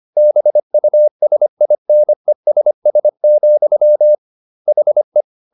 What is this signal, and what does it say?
Morse code, heard as a keyed tone, about 25 wpm, 600 Hz. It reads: BUSINESS, HE